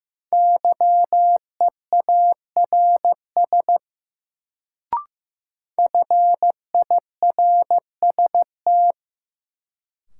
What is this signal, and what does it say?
Morse code, heard as a keyed tone, about 15 wpm, 700 Hz. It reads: YEARS FIRST
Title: Sets of 2 Words - New Word Drill - 15wpm